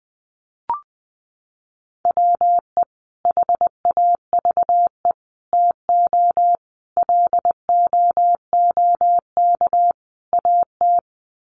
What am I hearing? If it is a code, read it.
Morse code, heard as a keyed tone, about 20 wpm, 700 Hz. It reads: WE HAVE TO LOOK AT
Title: Sets of 5 Words Encore2 20wpm